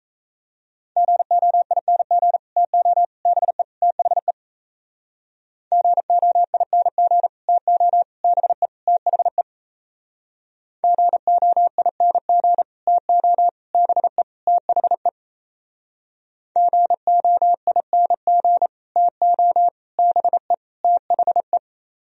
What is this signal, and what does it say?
Morse code, heard as a keyed tone, about 36 wpm, 700 Hz. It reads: GOING TO BE THE GOING TO BE THE GOING TO BE THE GOING TO BE THE